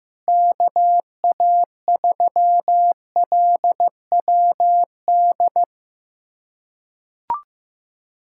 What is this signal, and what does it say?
Morse code, heard as a keyed tone, about 15 wpm, 700 Hz. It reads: KA3LWD
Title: USA Callsign DE Callsign 15wpm